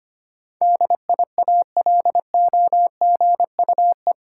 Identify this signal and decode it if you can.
Morse code, heard as a keyed tone, about 25 wpm, 700 Hz. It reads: DIALOGUE
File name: Eight Letter Words - 25wpm